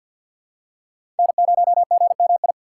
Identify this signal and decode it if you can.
Morse code, heard as a keyed tone, about 52 wpm, 700 Hz. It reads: N0GMI
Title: USA Call Signs - 5 Characters - 2x Character Spacing - 50wpm